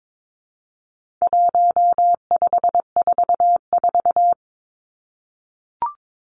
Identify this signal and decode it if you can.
Morse code, heard as a keyed tone, about 22 wpm, 700 Hz. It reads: 1544
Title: Numbers - 4-Digit 22wpm